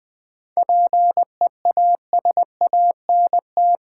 Morse code, heard as a keyed tone, about 21 wpm, 700 Hz. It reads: PEASANT
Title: 7 Letter Words - First 3 Letters Given - 20wpm